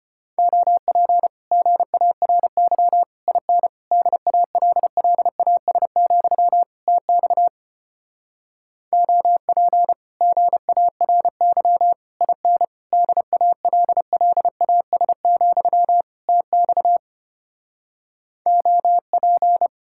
Morse code, heard as a keyed tone, about 33 wpm, 700 Hz. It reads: OP GARY IN DALLAS, TX OP GARY IN DALLAS, TX OP